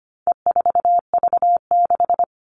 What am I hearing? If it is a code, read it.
Morse code, heard as a keyed tone, about 25 wpm, 700 Hz. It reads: E4V6
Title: All Cut Numbers - 4-Digit 25wpm